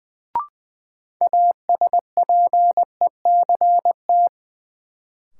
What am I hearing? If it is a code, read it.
Morse code, heard as a keyed tone, about 20 wpm, 700 Hz. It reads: ASPECT
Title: Sentences from Top 1300 Words - New Word Drill - 20wpm